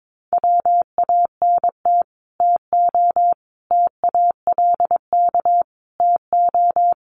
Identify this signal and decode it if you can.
Morse code, heard as a keyed tone, about 22 wpm, 700 Hz. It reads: WANT TO TALK TO
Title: Sets of 5 Words Encore 22wpm